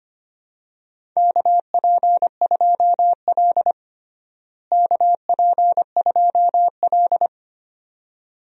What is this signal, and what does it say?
Morse code, heard as a keyed tone, about 25 wpm, 700 Hz. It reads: KP2L KP2L
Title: DXCC-100 Call Signs - 4 Characters - Repeated 2x - 25wpm